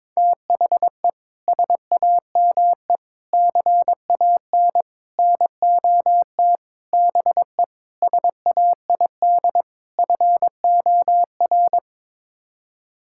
Morse code, heard as a keyed tone, about 22 wpm, 700 Hz. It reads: THE SAME CAN NOT BE SAID FOR
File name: practice / 22wpm / Sets of 7 Words - 22wpm